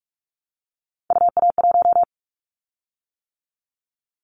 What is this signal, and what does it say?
Morse code, heard as a keyed tone, about 46 wpm, 700 Hz. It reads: UA1
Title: Callsign Prefixes for Top 100 DXCC - 45wpm